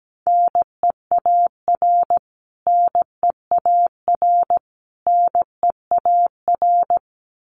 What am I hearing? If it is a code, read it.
Morse code, heard as a keyed tone, about 17 wpm, 700 Hz. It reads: NEAR NEAR NEAR
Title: Top 500 Words Repeated 17wpm